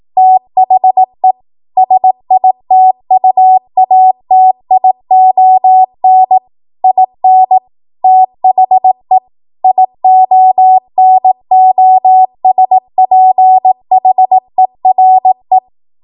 Morse code, heard as a keyed tone, about 18 wpm, 750 Hz. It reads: THE SITUATION IN THE IONOSPHERE
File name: Morse / 18WPM